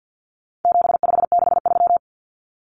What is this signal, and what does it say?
Morse code, heard as a keyed tone, about 50 wpm, 700 Hz. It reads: 7563
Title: Numbers - 4-Digit 50wpm